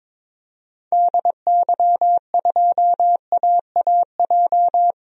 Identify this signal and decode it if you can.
Morse code, heard as a keyed tone, about 22 wpm, 700 Hz. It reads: DY2AAJ
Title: DXCC-100 Call Signs - 6 Characters - 22wpm